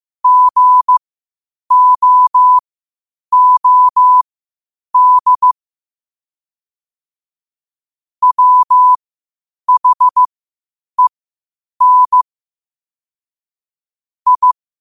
Morse code, heard as a keyed tone, about 15 wpm, spaced as for 8 wpm, 1000 Hz. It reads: GOOD WHEN I
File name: qotdfiles / 8wpm